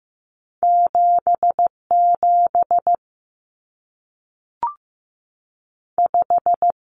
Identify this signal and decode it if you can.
Morse code, heard as a keyed tone, about 15 wpm, 700 Hz. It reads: 77 5
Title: Numbers - 2-Digit 15wpm